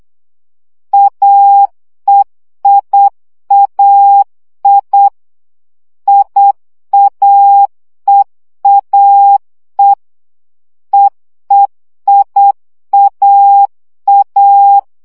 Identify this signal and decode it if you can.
Morse code, heard as a keyed tone, about 8 wpm, 800 Hz. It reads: AEIAI IAEAE EEIAA
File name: CW gyakorlas e i a 7 baud